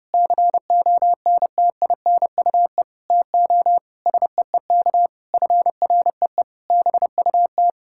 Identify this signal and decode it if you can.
Morse code, heard as a keyed tone, about 30 wpm, 700 Hz. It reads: CONTINUE TO SEEK FREE BUT
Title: Sentences from Top 1600 Words 30wpm